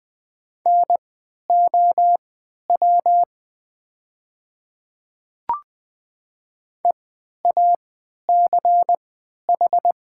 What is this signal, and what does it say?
Morse code, heard as a keyed tone, about 20 wpm, spaced as for 11 wpm, 700 Hz. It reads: NOW EACH
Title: 3x Character Spacing - Top 100 Words - 20wpm